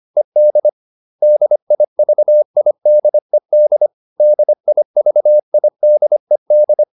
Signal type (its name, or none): Morse code